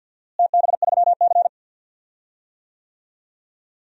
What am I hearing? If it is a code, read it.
Morse code, heard as a keyed tone, about 52 wpm, 700 Hz. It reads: TB3C
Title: DXCC-100 Call Signs - 4 Characters - Repeated 2x - 50wpm